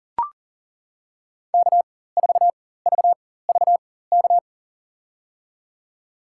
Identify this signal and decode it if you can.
Morse code, heard as a keyed tone, about 40 wpm, spaced as for 19 wpm, 700 Hz. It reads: K4VVK
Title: USA Call Signs - 5 Characters - 4x Character Spacing - 40wpm